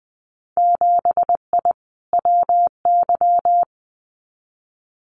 Morse code, heard as a keyed tone, about 20 wpm, 700 Hz. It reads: 7I WY
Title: Winter Field Day Exchanges Encore 20wpm